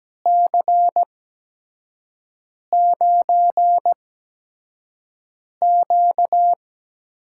Morse code, heard as a keyed tone, about 17 wpm, spaced as for 5 wpm, 700 Hz. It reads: C9Q
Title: USA Call Signs - 8x Character Spacing - 17wpm